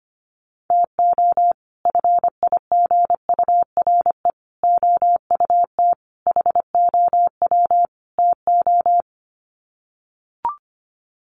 Morse code, heard as a keyed tone, about 25 wpm, 700 Hz. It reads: TO FIGURE OUT HOW TO
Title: Sets of 5 Words 25wpm